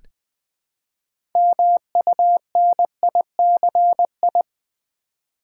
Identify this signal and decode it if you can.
Morse code, heard as a keyed tone, about 20 wpm, 700 Hz. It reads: MUNICI
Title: Build A Word - 3 Letters At A Time - 9-Letter Words - 20wpm